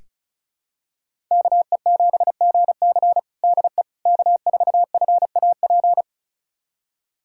Morse code, heard as a keyed tone, about 36 wpm, 700 Hz. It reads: KE7GC DE K4FAP
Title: USA Callsign DE Callsign 35wpm